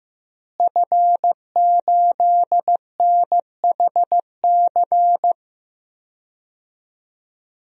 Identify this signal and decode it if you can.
Morse code, heard as a keyed tone, about 15 wpm, 700 Hz. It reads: F8NHC
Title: DXCC-100 Call Signs - 5 Characters - Repeated 3x - 15wpm